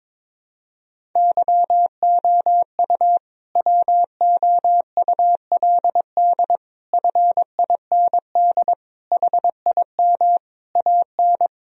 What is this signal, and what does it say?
Morse code, heard as a keyed tone, about 22 wpm, 700 Hz. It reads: YOU WOULD FIND HIM AN
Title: Sentences from Top 1000 Words - Review Part 1 - 22wpm